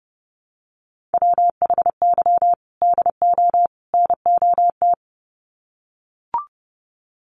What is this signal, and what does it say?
Morse code, heard as a keyed tone, about 30 wpm, 700 Hz. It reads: WHY DO NOT
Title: Sets of 3 Words 30wpm